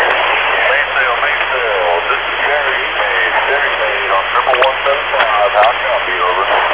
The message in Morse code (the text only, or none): none